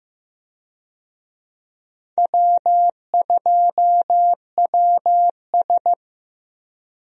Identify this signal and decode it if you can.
Morse code, heard as a keyed tone, about 15 wpm, 700 Hz. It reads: W2WS
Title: USA Call Signs - 4 Characters - Repeated 3x - 15wpm